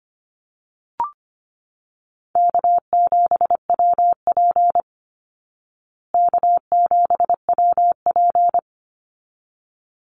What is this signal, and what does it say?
Morse code, heard as a keyed tone, about 25 wpm, 700 Hz. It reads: K7WP K7WP
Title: Easy Call Signs Repeated 2x - 25wpm